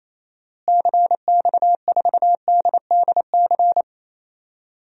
Morse code, heard as a keyed tone, about 28 wpm, 700 Hz. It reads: CX4DDC
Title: DXCC-100 Call Signs - 6 Characters - Repeated 3x - 28wpm